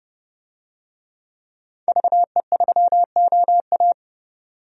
Morse code, heard as a keyed tone, about 32 wpm, 700 Hz. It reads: VE3OA
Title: DXCC-100 Call Signs - 5 Characters - 30wpm